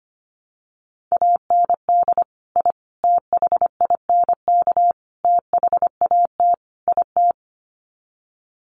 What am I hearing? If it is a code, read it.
Morse code, heard as a keyed tone, about 25 wpm, 700 Hz. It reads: AND I THINK THAT IT
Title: Sets of 5 Words Encore2 25wpm